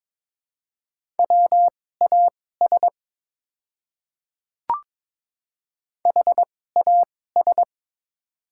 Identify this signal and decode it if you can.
Morse code, heard as a keyed tone, about 22 wpm, spaced as for 16 wpm, 700 Hz. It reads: WAS HAS
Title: 2x Character Spacing - Top 100 Words - 22wpm